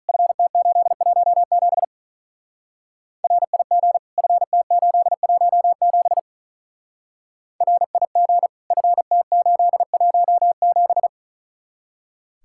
Morse code, heard as a keyed tone, about 44 wpm, 700 Hz. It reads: FT817 RIG FT817 RIG FT817
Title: Speed-Racing Common QSO Phrases - 35wpm